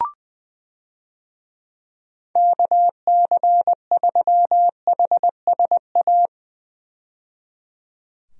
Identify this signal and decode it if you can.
Morse code, heard as a keyed tone, about 21 wpm, 700 Hz. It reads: KC3HSA
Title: USA Call Signs - 6 Characters - 20wpm